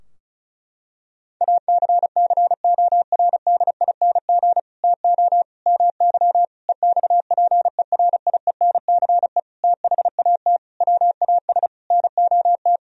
Morse code, heard as a keyed tone, about 34 wpm, 700 Hz. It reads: ACCORDING TO MY EXPERIENCE THAT WAS NOT
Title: Sentences from Top 1000 Words - Review Part 4 - 35wpm